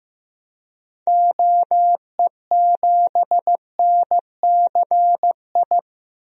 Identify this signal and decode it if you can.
Morse code, heard as a keyed tone, about 15 wpm, 700 Hz. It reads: OE7NCI
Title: DXCC-100 Call Signs - 6 Characters - 15wpm